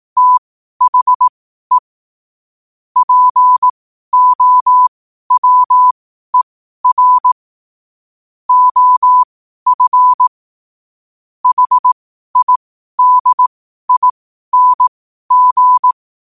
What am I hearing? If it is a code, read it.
Morse code, heard as a keyed tone, about 18 wpm, spaced as for 12 wpm, 1000 Hz. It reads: THE POWER OF HIDING
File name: qotdfiles / 12wpm